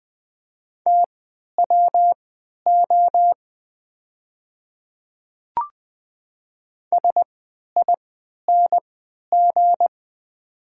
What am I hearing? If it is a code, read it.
Morse code, heard as a keyed tone, about 20 wpm, spaced as for 11 wpm, 700 Hz. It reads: TWO SING
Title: 3x Character Spacing - Top 500 Words - 20wpm